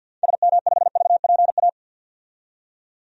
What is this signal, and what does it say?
Morse code, heard as a keyed tone, about 48 wpm, 700 Hz. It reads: SM5VPA
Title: DXCC-100 Call Signs - 6 Characters - 50wpm